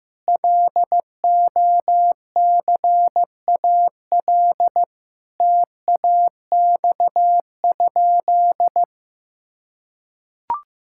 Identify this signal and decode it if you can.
Morse code, heard as a keyed tone, about 15 wpm, 700 Hz. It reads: LOCAL TAX?
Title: Sentences from Top 1000 Words - Review Part 5 - 15wpm